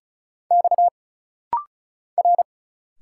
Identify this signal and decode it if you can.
Morse code, heard as a keyed tone, about 34 wpm, 700 Hz. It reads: X R
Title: Single Letters - Rapid-Fire 35wpm